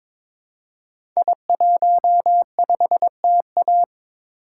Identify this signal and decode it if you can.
Morse code, heard as a keyed tone, about 23 wpm, 700 Hz. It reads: I15TA
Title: Code Groups - 5 Letter:Numbers - 22wpm